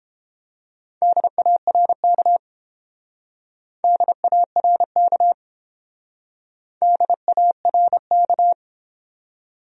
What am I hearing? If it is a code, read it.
Morse code, heard as a keyed tone, about 30 wpm, 700 Hz. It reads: DARK DARK DARK